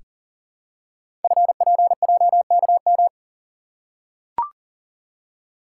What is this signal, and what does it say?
Morse code, heard as a keyed tone, about 42 wpm, 700 Hz. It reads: FPJKM
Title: Code Groups - 5 Letter:Numbers - 40wpm